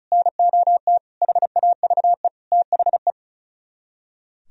Morse code, heard as a keyed tone, about 46 wpm, 700 Hz. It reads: NOT HAVE THE